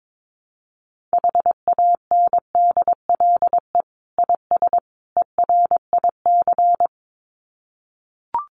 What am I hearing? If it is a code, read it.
Morse code, heard as a keyed tone, about 22 wpm, 700 Hz. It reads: HANDLE IS ERIC
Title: Common QSO Phrases 22wpm